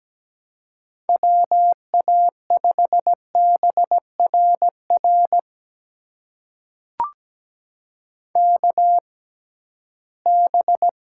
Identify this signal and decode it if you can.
Morse code, heard as a keyed tone, about 17 wpm, spaced as for 13 wpm, 700 Hz. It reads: WA5BRR KB
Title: USA Call Signs - 6 Characters - 6x Character Spacing - 17wpm